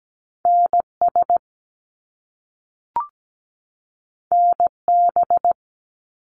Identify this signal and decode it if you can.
Morse code, heard as a keyed tone, about 17 wpm, 700 Hz. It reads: NS NB
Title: Canadian Provinces and Territories 17wpm